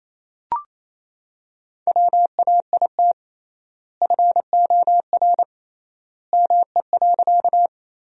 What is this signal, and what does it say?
Morse code, heard as a keyed tone, about 27 wpm, 700 Hz. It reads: WAIT FOR ME.
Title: 3x Word-Spacing - Sentences from Top 500 Words - 28wpm